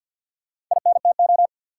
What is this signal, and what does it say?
Morse code, heard as a keyed tone, about 51 wpm, 700 Hz. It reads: INTO
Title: Sentences from Top 600 Words - New Word Drill - 50wpm